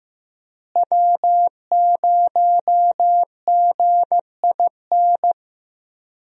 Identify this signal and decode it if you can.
Morse code, heard as a keyed tone, about 15 wpm, 700 Hz. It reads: W0GIN
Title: USA Call Signs - 5 Characters - Repeated 2x - 15wpm